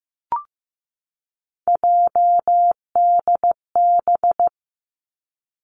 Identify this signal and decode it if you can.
Morse code, heard as a keyed tone, about 15 wpm, 700 Hz. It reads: JDB
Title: Three Challenging Letter-Number Combinations 15wpm